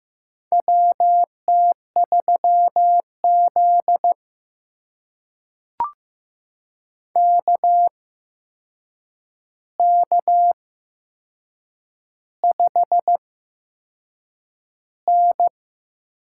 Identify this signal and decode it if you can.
Morse code, heard as a keyed tone, about 15 wpm, spaced as for 6 wpm, 700 Hz. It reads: WT3Z KK5N
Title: USA Call Signs - 4 Characters - 8x Character Spacing - 15wpm